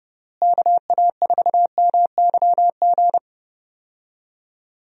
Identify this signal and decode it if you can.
Morse code, heard as a keyed tone, about 32 wpm, 700 Hz. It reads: KA4MYG
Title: USA Call Signs - 6 Characters - Repeated 3x - 30wpm